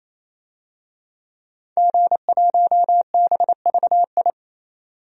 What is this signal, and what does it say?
Morse code, heard as a keyed tone, about 28 wpm, 700 Hz. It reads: G1BVI
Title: DXCC-100 Call Signs - 5 Characters - 28wpm